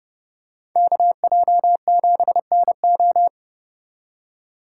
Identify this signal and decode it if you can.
Morse code, heard as a keyed tone, about 29 wpm, 700 Hz. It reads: KJ7NO